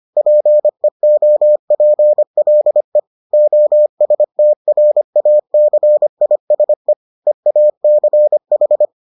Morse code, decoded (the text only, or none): PEOPLE OSTRACISE EACH